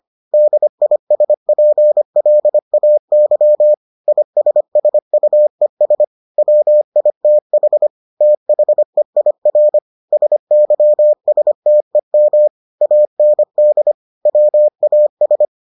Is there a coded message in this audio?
Yes: Morse code from 0.3 to 15.4 s, heard as a keyed tone, about 25 wpm, 600 Hz. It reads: DISPLAY ISSUES WITH THEIR SYSTEM AND WAS